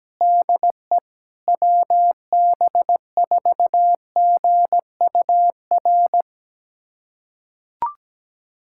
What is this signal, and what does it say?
Morse code, heard as a keyed tone, about 17 wpm, 700 Hz. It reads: DE WB4GUR
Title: USA Callsign DE Callsign 17wpm